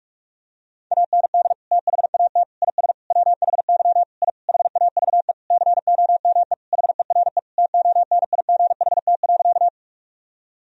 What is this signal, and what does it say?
Morse code, heard as a keyed tone, about 46 wpm, 700 Hz. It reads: AND THAT IS WHY I HAVE COME HERE TONIGHT.
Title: Sentences from Top 1000 Words - Review Part 3 - 45wpm